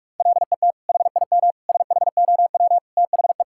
Morse code, heard as a keyed tone, about 41 wpm, 700 Hz. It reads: LET HIM SHOW THE
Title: Sentences from Top 1600 Words 45wpm